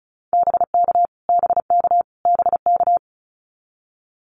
Morse code, heard as a keyed tone, about 34 wpm, 700 Hz. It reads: BK BK BK
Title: Abbreviations Repeated 3x - 35wpm